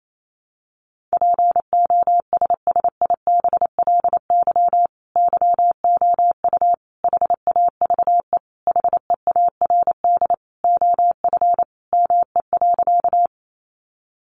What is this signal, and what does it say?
Morse code, heard as a keyed tone, about 28 wpm, 700 Hz. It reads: POSSIBLY YOU HAVE HEARD OF ME.
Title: Sentences from Top 1300 Words 28wpm